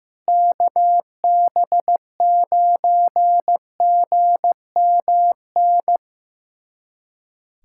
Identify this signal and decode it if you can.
Morse code, heard as a keyed tone, about 15 wpm, 700 Hz. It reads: KB9GMN
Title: USA Callsign DE Callsign - 6 Characters - 15wpm